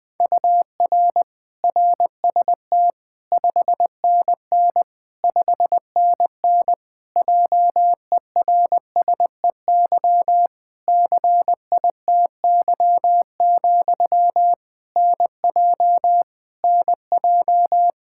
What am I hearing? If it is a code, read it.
Morse code, heard as a keyed tone, about 20 wpm, 700 Hz. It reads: UR RST 5NN 5NN JERSEY CITY, NJ NJ
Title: Practice QSO - Exchange 1 - Beginning - 20wpm